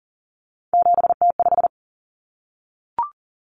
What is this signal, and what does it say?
Morse code, heard as a keyed tone, about 40 wpm, 700 Hz. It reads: 7T5
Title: Common Cut Numbers - 3-Digit 40wpm